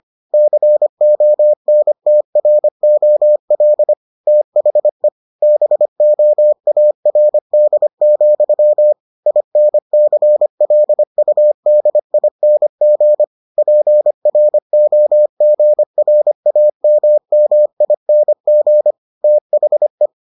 Morse code, heard as a keyed tone, about 25 wpm, 600 Hz. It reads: CONTROL THE BOARD, INCLUDING PROGRAMMING THE